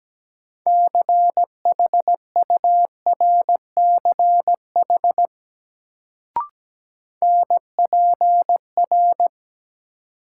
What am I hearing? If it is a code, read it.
Morse code, heard as a keyed tone, about 17 wpm, 700 Hz. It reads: CHURCH NPR
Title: Sets of 5 Words - New Word Drill - 17wpm